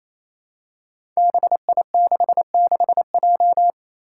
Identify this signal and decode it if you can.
Morse code, heard as a keyed tone, about 28 wpm, 700 Hz. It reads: BI66J
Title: Five Challenging Letter-Number Combinations 28wpm